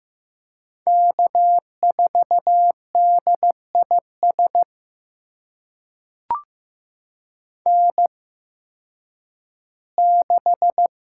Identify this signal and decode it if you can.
Morse code, heard as a keyed tone, about 15 wpm, spaced as for 10 wpm, 700 Hz. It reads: K4DIS N6
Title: USA Call Signs - 5 Characters - 8x Character Spacing - 15wpm